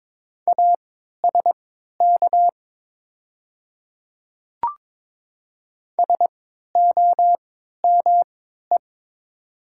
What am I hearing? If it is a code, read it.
Morse code, heard as a keyed tone, about 22 wpm, spaced as for 12 wpm, 700 Hz. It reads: ASK SOME
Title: 3x Character Spacing - Top 200 Words - 22wpm